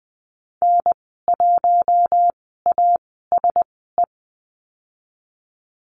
Morse code, heard as a keyed tone, about 20 wpm, spaced as for 15 wpm, 700 Hz. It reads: N1ASE
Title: 042 Lesson 8g - Number 1 - Call Signs 20wpm